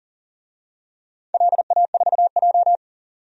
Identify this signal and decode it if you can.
Morse code, heard as a keyed tone, about 40 wpm, 700 Hz. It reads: LA4J
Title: DXCC-100 Call Signs - 4 Characters - 40wpm